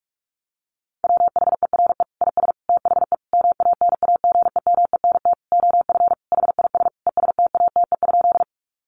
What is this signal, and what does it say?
Morse code, heard as a keyed tone, about 44 wpm, 700 Hz. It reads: WHERE IS THE MANAGEMENT OF HIS ESTATE?
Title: Sentences from Top 1200 Words 45wpm